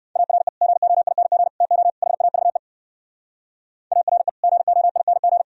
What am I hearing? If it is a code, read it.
Morse code, heard as a keyed tone, about 68 wpm, 700 Hz. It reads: ARE GOING TO HAVE ARE GOING